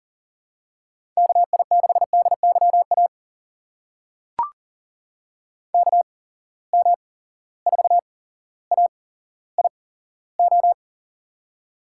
Morse code, heard as a keyed tone, about 40 wpm, spaced as for 17 wpm, 700 Hz. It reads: KI6DYA KM4AIO
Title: USA Call Signs - 6 Characters - 8x Character Spacing - 40wpm